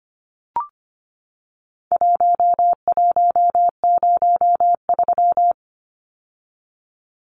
Morse code, heard as a keyed tone, about 25 wpm, 700 Hz. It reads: 1103
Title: Numbers - Contest Style - 4-Digit 25wpm